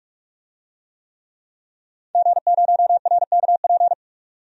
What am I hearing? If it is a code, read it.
Morse code, heard as a keyed tone, about 42 wpm, 700 Hz. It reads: G0RKP
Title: DXCC-100 Call Signs - 5 Characters - 45wpm